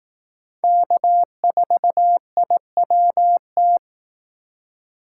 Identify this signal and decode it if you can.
Morse code, heard as a keyed tone, about 18 wpm, 700 Hz. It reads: K4IWT
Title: Speed Racing - USA 1x3 Call Signs 15wpm